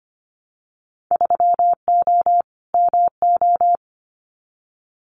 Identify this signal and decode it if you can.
Morse code, heard as a keyed tone, about 25 wpm, 700 Hz. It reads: 3O MO
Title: Winter Field Day Exchanges Encore 25wpm